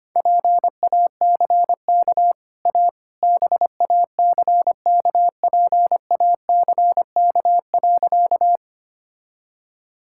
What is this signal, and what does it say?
Morse code, heard as a keyed tone, about 25 wpm, 700 Hz. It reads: PACK A BACKPACK.
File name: Easy Sentences with Hints - 25wpm